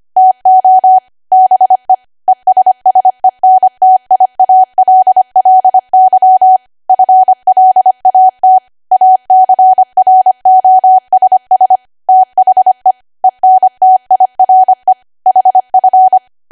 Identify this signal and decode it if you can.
Morse code, heard as a keyed tone, about 25 wpm, 750 Hz. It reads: TO BE ESSENTIALLY FLAT ACROSS THE ENTIRE HF